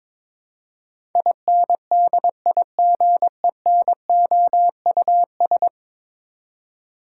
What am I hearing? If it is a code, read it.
Morse code, heard as a keyed tone, about 22 wpm, 700 Hz. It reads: INDIGENOUS
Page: morse-code-ninja-practice-22wpm